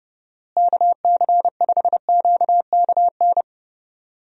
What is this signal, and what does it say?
Morse code, heard as a keyed tone, about 30 wpm, 700 Hz. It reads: KC5QKN
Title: USA Call Signs - 6 Characters - Repeated 3x - 30wpm